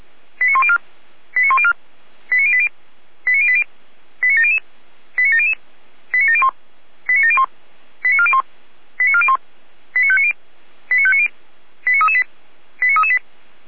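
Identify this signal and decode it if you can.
Five-tone selective call (ZVEI), heard as quick runs of five tones, tones of about 70 ms. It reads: 89104 89104 89080 89080 89700 89700 89701 89701 89401 89401 89590 89590 89338 89338
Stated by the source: ZVEI five-tone selective call received on SDR.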